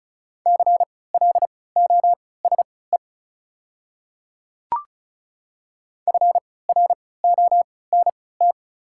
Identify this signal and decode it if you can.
Morse code, heard as a keyed tone, about 36 wpm, spaced as for 20 wpm, 690 Hz. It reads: CLOSE FRONT
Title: 3x Character Spacing - Top 500 Words - 35wpm